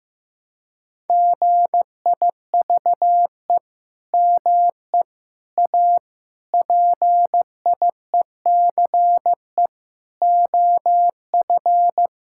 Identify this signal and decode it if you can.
Morse code, heard as a keyed tone, about 15 wpm, 700 Hz. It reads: GIVE ME A PIECE OF
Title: Sentences from Top 1600 Words 15wpm